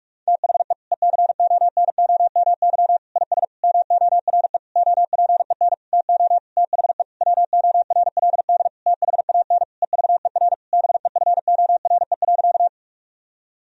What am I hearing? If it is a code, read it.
Morse code, heard as a keyed tone, about 46 wpm, 700 Hz. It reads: THE ECONOMY IS MORE OPEN TO THE WORLD THAN EVER BEFORE.
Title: Sentences from Top 1000 Words - Review Part 3 - 45wpm